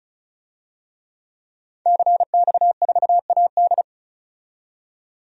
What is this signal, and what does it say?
Morse code, heard as a keyed tone, about 36 wpm, 700 Hz. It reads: CX4AD
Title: DXCC-100 Call Signs - 5 Characters - 35wpm